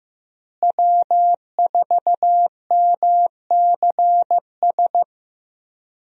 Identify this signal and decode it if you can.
Morse code, heard as a keyed tone, about 15 wpm, 700 Hz. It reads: W4MCS
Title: USA Call Signs - 5 Characters - Repeated 3x - 15wpm